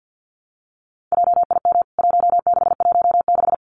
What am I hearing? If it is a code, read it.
Morse code, heard as a keyed tone, about 48 wpm, 700 Hz. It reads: JIM 1616
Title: CWT Exchange 50wpm